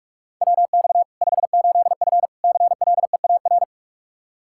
Morse code, heard as a keyed tone, about 46 wpm, 700 Hz. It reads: WX 58F CLEAR
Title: Practice QSO - Exchange 2 - Info - 45wpm